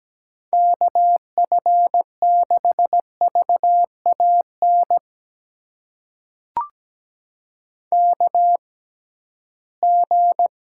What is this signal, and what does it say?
Morse code, heard as a keyed tone, about 17 wpm, spaced as for 13 wpm, 700 Hz. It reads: KF6VAN KG
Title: USA Call Signs - 6 Characters - 6x Character Spacing - 17wpm